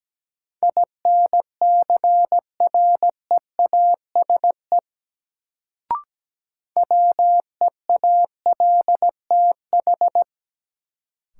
Morse code, heard as a keyed tone, about 17 wpm, 700 Hz. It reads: INCREASE WEALTH
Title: Sentences from Top 1200 Words - New Word Drill Alt - 17wpm